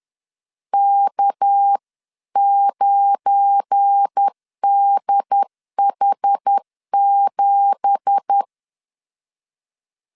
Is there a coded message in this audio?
Yes: Morse code from 0.7 to 8.4 s, heard as a keyed tone, about 10 wpm, 800 Hz. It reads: K9DH7